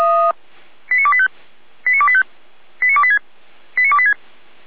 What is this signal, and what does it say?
Five-tone selective call (ZVEI), heard as quick runs of five tones, tones of about 70 ms. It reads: 89285 89285 89286 89286